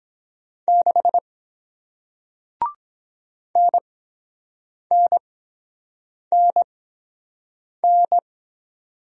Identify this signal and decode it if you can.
Morse code, heard as a keyed tone, about 23 wpm, spaced as for 7 wpm, 700 Hz. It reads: 6 NNNN